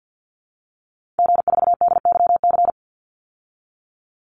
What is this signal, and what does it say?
Morse code, heard as a keyed tone, about 50 wpm, 700 Hz. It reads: G4DYC
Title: Call Signs Repeated 3x - 50wpm